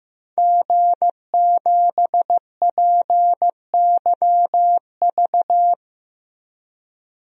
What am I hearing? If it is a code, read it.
Morse code, heard as a keyed tone, about 15 wpm, 700 Hz. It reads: G7PYV
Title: DXCC-100 Call Signs - 5 Characters - Repeated 2x - 15wpm